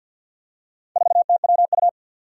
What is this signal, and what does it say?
Morse code, heard as a keyed tone, about 51 wpm, 700 Hz. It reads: 4TWU
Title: Four Challenging Letter-Number Combinations 50wpm